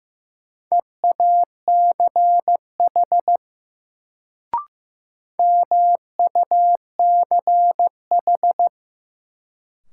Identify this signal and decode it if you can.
Morse code, heard as a keyed tone, about 15 wpm, 700 Hz. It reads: EACH MUCH
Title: Sets of 2 Words - New Word Drill - 15wpm